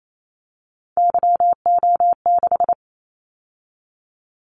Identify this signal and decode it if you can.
Morse code, heard as a keyed tone, about 28 wpm, 700 Hz. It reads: YO6
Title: Callsign Prefixes for Top 25 DXCC - 28wpm